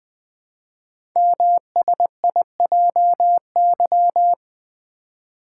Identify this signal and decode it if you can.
Morse code, heard as a keyed tone, about 20 wpm, 700 Hz. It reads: MSIJY
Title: Code Groups - 5 Letters - 20wpm